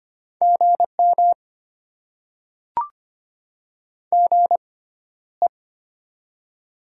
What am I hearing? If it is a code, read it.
Morse code, heard as a keyed tone, about 25 wpm, spaced as for 13 wpm, 700 Hz. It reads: GM GE
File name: Top 20 QSO Elements - 6x Character Spacing - 25wpm